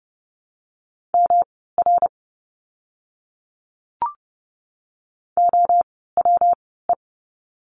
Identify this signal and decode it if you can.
Morse code, heard as a keyed tone, about 29 wpm, spaced as for 17 wpm, 700 Hz. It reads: MR OWE
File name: course / 30:10wpm / 115 Lesson 19f - Letter w - Words 30wpm